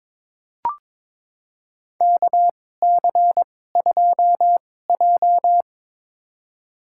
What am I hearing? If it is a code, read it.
Morse code, heard as a keyed tone, about 22 wpm, spaced as for 16 wpm, 700 Hz. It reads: KC2J
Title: USA Call Signs - 2x Character Spacing - 22wpm